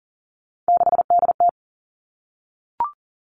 Morse code, heard as a keyed tone, about 38 wpm, 700 Hz. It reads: 6DT
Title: Three Challenging Letter-Number Combinations 40wpm